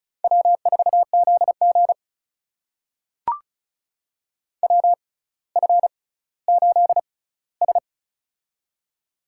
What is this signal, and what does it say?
Morse code, heard as a keyed tone, about 36 wpm, spaced as for 18 wpm, 700 Hz. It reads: W4ZG WF8S
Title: USA Call Signs - 6x Character Spacing - 35wpm